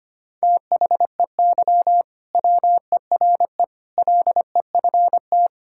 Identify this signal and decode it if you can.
Morse code, heard as a keyed tone, about 25 wpm, 700 Hz. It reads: THEY WERE LEFT
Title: Sentences from Top 1000 Words - Review Part 5 - 25wpm